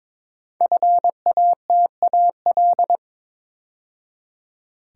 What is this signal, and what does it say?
Morse code, heard as a keyed tone, about 22 wpm, 700 Hz. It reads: FATAL
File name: 5 Letter Words - First 3 Letters Given - 22wpm